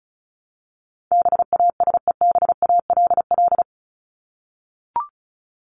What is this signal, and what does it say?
Morse code, heard as a keyed tone, about 34 wpm, 700 Hz. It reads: BASEBALL
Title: Top 1751-2000 Words 35wpm